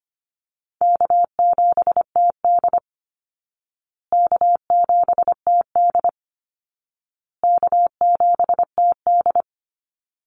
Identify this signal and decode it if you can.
Morse code, heard as a keyed tone, about 25 wpm, 700 Hz. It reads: K7TD K7TD K7TD
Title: Easy Call Signs Repeated 3x - 25wpm